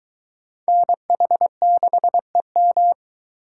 Morse code, heard as a keyed tone, about 20 wpm, 700 Hz. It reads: NH6EM